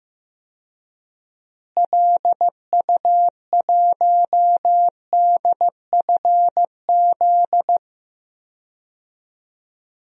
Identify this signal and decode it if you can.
Morse code, heard as a keyed tone, about 15 wpm, 700 Hz. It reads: LU1DFZ
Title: DXCC-100 Call Signs - 6 Characters - Repeated 2x - 15wpm